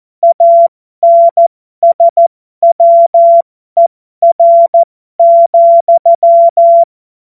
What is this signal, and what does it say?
Morse code, heard as a keyed tone, about 14 wpm, spaced as for 12 wpm, 680 Hz. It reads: ANSWER,
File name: Morsekurs / Witze